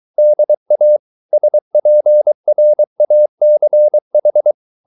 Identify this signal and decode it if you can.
Morse code, heard as a keyed tone, about 23 wpm, 600 Hz. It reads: DA SPRACH